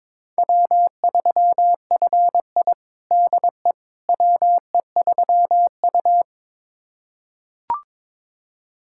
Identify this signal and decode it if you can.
Morse code, heard as a keyed tone, about 22 wpm, 700 Hz. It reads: W3FI DE WE3U
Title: USA Callsign DE Callsign - 4 Characters - 22wpm